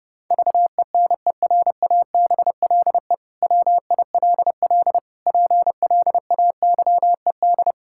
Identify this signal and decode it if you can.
Morse code, heard as a keyed tone, about 29 wpm, 700 Hz. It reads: VENERABLE WILL PLAYED
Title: Pangram Challenge v2 - 30wpm